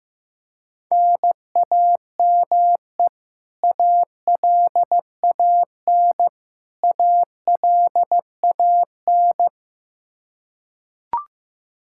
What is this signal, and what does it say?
Morse code, heard as a keyed tone, about 15 wpm, 700 Hz. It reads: NAME ALAN ALAN
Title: Practice QSO - Exchange 1 - Beginning - 15wpm